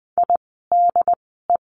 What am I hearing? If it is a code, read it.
Morse code, heard as a keyed tone, about 19 wpm, spaced as for 14 wpm, 700 Hz. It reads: IDE
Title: 108 Lesson 18f - Letter m - Words 20wpm